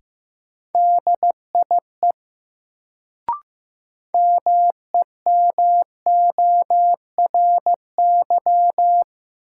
Morse code, heard as a keyed tone, about 15 wpm, 700 Hz. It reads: DIE MEMORY
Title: Sentences from Top 1300 Words - New Word Drill Alt - 15wpm